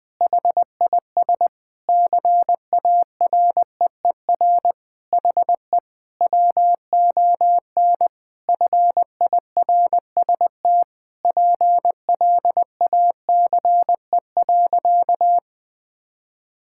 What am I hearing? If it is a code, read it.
Morse code, heard as a keyed tone, about 21 wpm, 700 Hz. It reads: HIS CAREER HE WON FIRST PLACE.
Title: Sentences with 7-Word Phrases - 20wpm